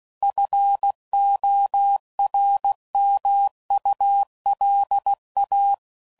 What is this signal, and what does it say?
Morse code, heard as a keyed tone, about 16 wpm, 800 Hz. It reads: FORMULA
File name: CW Code Groups @15WPM / Technical Subjects / XC